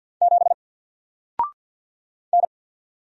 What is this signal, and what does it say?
Morse code, heard as a keyed tone, about 50 wpm, 700 Hz. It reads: 7 N
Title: Single Letter-Number Alt 50wpm